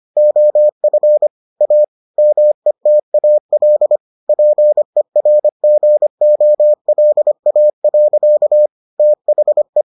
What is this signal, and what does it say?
Morse code, heard as a keyed tone, about 25 wpm, 600 Hz. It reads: OF A METAL PERGOLA. THE